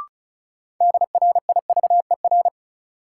Morse code, heard as a keyed tone, about 36 wpm, 700 Hz. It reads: DRIVER
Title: Sentences from Top 1000 Words - New Word Drill - 35wpm